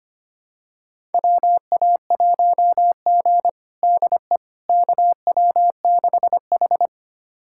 Morse code, heard as a keyed tone, about 25 wpm, 700 Hz. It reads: WA1G DE KW6H
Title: USA Callsign DE Callsign - 4 Characters - 25wpm